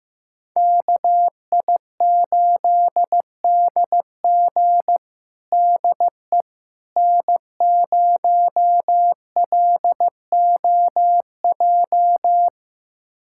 Morse code, heard as a keyed tone, about 15 wpm, 700 Hz. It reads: KI8DG DE N0LOJ
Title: USA Callsign DE Callsign - 5 Characters - 15wpm